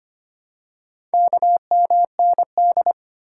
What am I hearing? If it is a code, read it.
Morse code, heard as a keyed tone, about 25 wpm, 700 Hz. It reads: KMND
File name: practice / 25wpm / Four Letter-Number Combinations - 25wpm